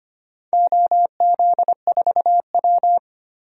Morse code, heard as a keyed tone, about 25 wpm, 700 Hz. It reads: OZ4W
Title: DXCC-100 Call Signs - 4 Characters - 25wpm